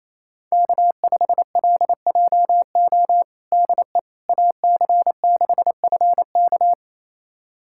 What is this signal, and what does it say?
Morse code, heard as a keyed tone, about 28 wpm, 700 Hz. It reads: K5LJO DE AC6FK
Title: USA Callsign DE Callsign - 5 Characters - 28wpm